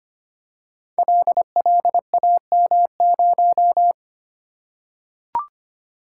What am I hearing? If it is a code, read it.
Morse code, heard as a keyed tone, about 25 wpm, 700 Hz. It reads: LLAM0
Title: Code Groups - 5 Letter:Numbers - 25wpm